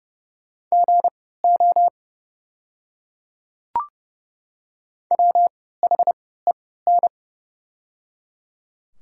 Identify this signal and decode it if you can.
Morse code, heard as a keyed tone, about 30 wpm, spaced as for 17 wpm, 700 Hz. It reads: GO WHEN
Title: 3x Character Spacing - Top 100 Words - 30wpm